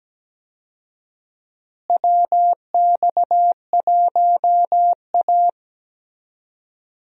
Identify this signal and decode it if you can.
Morse code, heard as a keyed tone, about 17 wpm, 700 Hz. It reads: WX1A
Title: USA Call Signs - 4 Characters - Repeated 2x - 17wpm